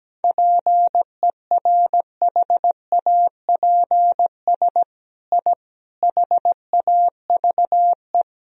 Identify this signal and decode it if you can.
Morse code, heard as a keyed tone, about 17 wpm, 700 Hz. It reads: PERHAPS I HAVE
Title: Sentences from Top 1000 Words - Review Part 2 - 17wpm